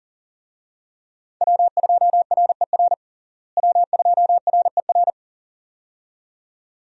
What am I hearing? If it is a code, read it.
Morse code, heard as a keyed tone, about 40 wpm, 700 Hz. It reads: W2RER W2RER
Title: USA Call Signs - 5 Characters - Repeated 2x - 40wpm